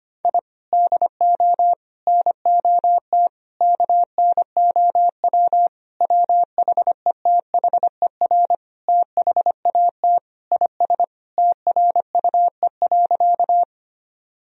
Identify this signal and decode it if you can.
Morse code, heard as a keyed tone, about 25 wpm, 700 Hz. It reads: I DO NOT KNOW WHETHER THAT IS TRUE.
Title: Sentences from Top 1000 Words - Review Part 3 - 25wpm